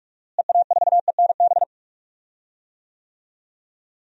Morse code, heard as a keyed tone, about 46 wpm, 700 Hz. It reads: EA4ENB
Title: DXCC-100 Call Signs - 6 Characters - Repeated 3x - 45wpm